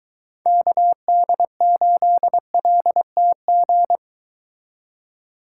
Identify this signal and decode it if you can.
Morse code, heard as a keyed tone, about 23 wpm, 700 Hz. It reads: KD8LTG